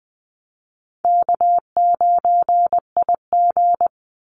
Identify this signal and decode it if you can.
Morse code, heard as a keyed tone, about 20 wpm, 700 Hz. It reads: K9IG
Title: Easy Call Signs - 20wpm